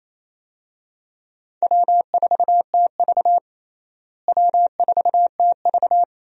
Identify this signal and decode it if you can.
Morse code, heard as a keyed tone, about 28 wpm, 700 Hz. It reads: W4TV W4TV
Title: USA Call Signs - 4 Characters - Repeated 2x - 28wpm